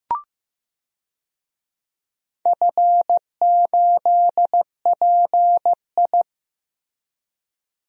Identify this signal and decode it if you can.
Morse code, heard as a keyed tone, about 15 wpm, 700 Hz. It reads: F8PI
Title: DXCC-100 Call Signs - 4 Characters - Repeated 2x - 15wpm